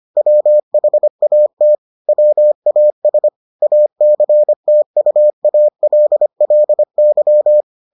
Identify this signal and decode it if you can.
Morse code, heard as a keyed tone, about 25 wpm, 600 Hz. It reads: WHAT WAS ACTUALLY